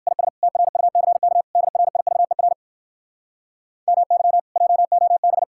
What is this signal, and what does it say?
Morse code, heard as a keyed tone, about 67 wpm, 700 Hz. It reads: IS TRUCK DRIVER MY JOB